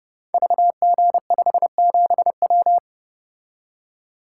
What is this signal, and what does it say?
Morse code, heard as a keyed tone, about 29 wpm, 700 Hz. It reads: VG57W
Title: Code Groups - 5 Letter:Numbers - 30wpm